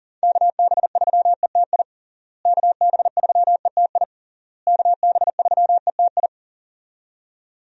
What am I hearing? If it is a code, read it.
Morse code, heard as a keyed tone, about 40 wpm, 700 Hz. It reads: KB3ETI KB3ETI KB3ETI
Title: USA Call Signs - 6 Characters - Repeated 3x - 40wpm